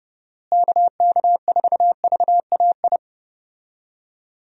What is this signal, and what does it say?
Morse code, heard as a keyed tone, about 32 wpm, 700 Hz. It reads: KK4VAI
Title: USA Call Signs - 6 Characters - Repeated 2x - 30wpm